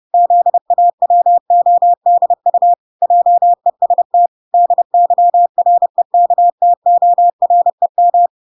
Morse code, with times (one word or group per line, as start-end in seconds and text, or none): 0.1-2.7 ZAWODU
3.0-4.3 JEST
4.5-8.3 DYREKTOREM